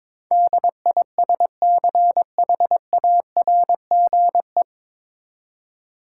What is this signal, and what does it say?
Morse code, heard as a keyed tone, about 22 wpm, 700 Hz. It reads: DISCHARGE
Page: Morse Code Ninja practice 22wpm